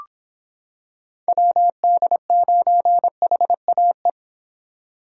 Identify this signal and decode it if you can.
Morse code, heard as a keyed tone, about 25 wpm, 700 Hz. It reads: WD9HAE